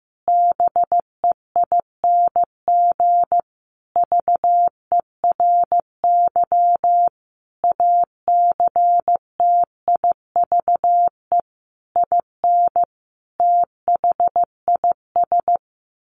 Morse code, heard as a keyed tone, about 15 wpm, 700 Hz. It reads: BEING VERY ACTIVE IN THIS
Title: Sentences from Top 1500 Words 15wpm